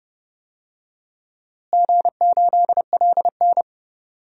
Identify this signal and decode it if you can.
Morse code, heard as a keyed tone, about 30 wpm, 700 Hz. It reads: G8LN
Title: DXCC-100 Call Signs - 4 Characters - Repeated 2x - 30wpm